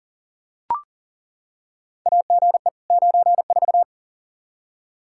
Morse code, heard as a keyed tone, about 40 wpm, 690 Hz. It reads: AGE 94
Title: Practice QSO - Exchange 3 - More Info - 40wpm